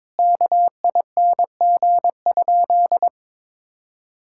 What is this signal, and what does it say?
Morse code, heard as a keyed tone, about 22 wpm, 700 Hz. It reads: KING?
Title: Sentences from Top 1600 Words 22wpm